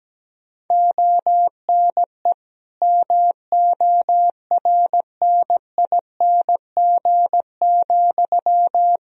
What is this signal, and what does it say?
Morse code, heard as a keyed tone, about 17 wpm, 700 Hz. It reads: ONE MORNING,
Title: Sentences from Top 1000 Words - Review Part 2 - 17wpm